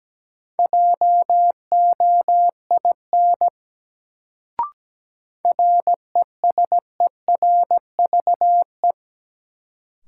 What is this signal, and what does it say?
Morse code, heard as a keyed tone, about 17 wpm, 700 Hz. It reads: JOIN RESERVE
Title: Sentences from Top 1000 Words - Review Part 6 - New Word Drill - 17wpm